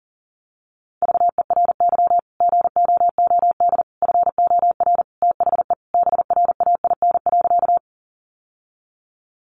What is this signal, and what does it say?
Morse code, heard as a keyed tone, about 40 wpm, 700 Hz. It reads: VERY GOOD FOR THE BRAIN.
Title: Sentences from Top 1100 Words 40wpm